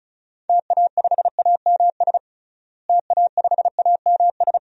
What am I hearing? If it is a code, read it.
Morse code, heard as a keyed tone, about 34 wpm, 700 Hz. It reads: TA5AMS TA5AMS
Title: DXCC-100 Call Signs - 6 Characters - Repeated 3x - 35wpm